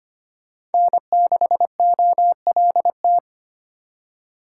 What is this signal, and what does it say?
Morse code, heard as a keyed tone, about 32 wpm, 700 Hz. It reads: N6OLT